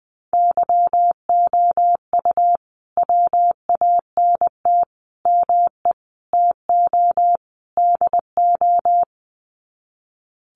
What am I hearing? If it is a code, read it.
Morse code, heard as a keyed tone, about 20 wpm, 700 Hz. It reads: YOU WANT ME TO DO
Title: Sets of 5 Words Encore2 20wpm